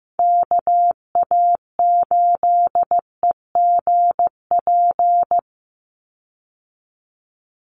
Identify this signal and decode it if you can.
Morse code, heard as a keyed tone, about 15 wpm, 700 Hz. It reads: KA8EGP
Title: USA Call Signs 15wpm